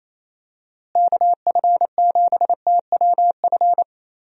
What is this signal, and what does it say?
Morse code, heard as a keyed tone, about 28 wpm, 700 Hz. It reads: KF7TWF
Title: USA Call Signs - 6 Characters - Repeated 3x - 28wpm